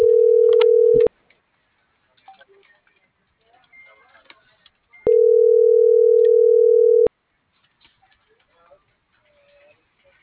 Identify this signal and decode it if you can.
Ringback tone (North America), 440 + 480 Hz.